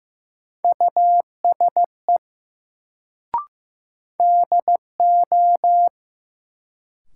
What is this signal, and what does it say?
Morse code, heard as a keyed tone, about 15 wpm, 700 Hz. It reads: USE DO
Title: Sentences from Top 100 Words - New Word Drill - 15wpm